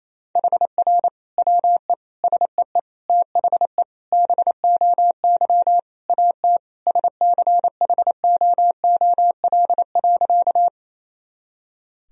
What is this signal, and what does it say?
Morse code, heard as a keyed tone, about 28 wpm, 700 Hz. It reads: HR WE SEE THE BOY AT SCHOOL.
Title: Sentences from Top 300 Words and Abbreviations 28wpm